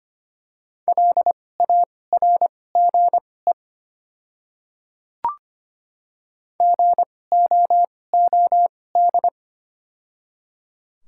Morse code, heard as a keyed tone, about 25 wpm, spaced as for 18 wpm, 700 Hz. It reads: LARGE GOOD
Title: 2x Character Spacing - Top 300 Words - 25wpm